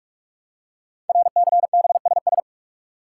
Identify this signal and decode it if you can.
Morse code, heard as a keyed tone, about 44 wpm, 700 Hz. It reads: RCBSS
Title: Code Groups - 5 Letters - 45wpm